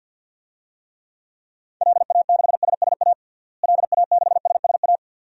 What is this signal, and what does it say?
Morse code, heard as a keyed tone, about 53 wpm, 700 Hz. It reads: LA6SSA LA6SSA
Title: DXCC-100 Call Signs - 6 Characters - Repeated 2x - 50wpm